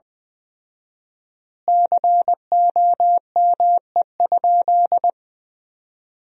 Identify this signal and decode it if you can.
Morse code, heard as a keyed tone, about 20 wpm, 700 Hz. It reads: COME?
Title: 4x Word-Spacing - Sentences from Top 200 Words - 20wpm